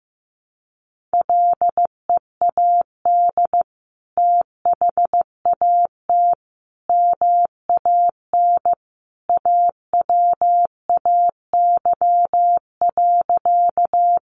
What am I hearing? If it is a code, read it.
Morse code, heard as a keyed tone, about 15 wpm, 700 Hz. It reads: LEAD THAT MAN AWAY.
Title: Sentences from Top 600 Words 15wpm